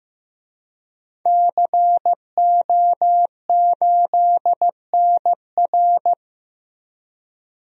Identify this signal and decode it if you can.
Morse code, heard as a keyed tone, about 15 wpm, 700 Hz. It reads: CO8NR
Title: DXCC-100 Call Signs - 5 Characters - Repeated 3x - 15wpm